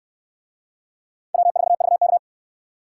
Morse code, heard as a keyed tone, about 68 wpm, 700 Hz. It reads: W4VK